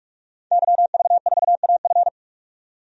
Morse code, heard as a keyed tone, about 46 wpm, 700 Hz. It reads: YV4AF
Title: DXCC-100 Call Signs - 5 Characters - 45wpm